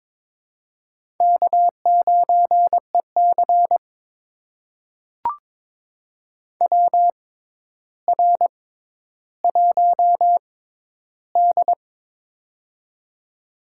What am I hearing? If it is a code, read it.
Morse code, heard as a keyed tone, about 22 wpm, spaced as for 11 wpm, 700 Hz. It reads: K9EC WR1D
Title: USA Call Signs - 4 Characters - 6x Character Spacing - 22wpm